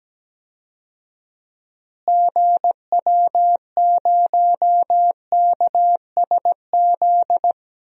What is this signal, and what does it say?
Morse code, heard as a keyed tone, about 17 wpm, 700 Hz. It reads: GW0KSZ
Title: DXCC-100 Call Signs - 6 Characters - Repeated 3x - 17wpm